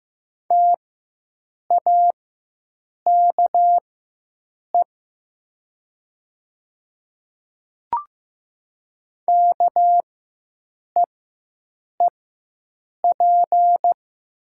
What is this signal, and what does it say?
Morse code, heard as a keyed tone, about 15 wpm, spaced as for 7 wpm, 700 Hz. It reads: TAKE KEEP
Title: 4x Character Spacing - Top 500 Words - 15wpm